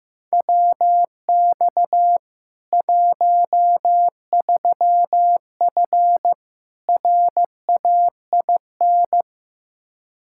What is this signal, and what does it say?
Morse code, heard as a keyed tone, about 15 wpm, 700 Hz. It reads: WX 13F RAIN
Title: Practice QSO - Exchange 2 - Info - 15wpm